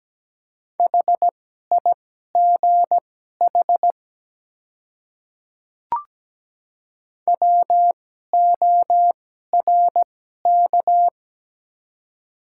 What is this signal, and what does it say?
Morse code, heard as a keyed tone, about 17 wpm, spaced as for 12 wpm, 700 Hz. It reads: HIGH WORK
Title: 2x Character Spacing - Top 200 Words - 17wpm